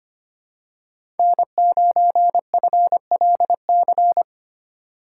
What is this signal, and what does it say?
Morse code, heard as a keyed tone, about 25 wpm, 700 Hz. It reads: N9FLC
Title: USA Call Signs - 2x Character Spacing - 25wpm